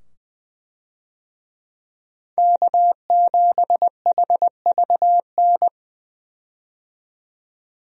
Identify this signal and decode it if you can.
Morse code, heard as a keyed tone, about 20 wpm, 700 Hz. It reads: K7HVN
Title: USA Call Signs - 5 Characters - 20wpm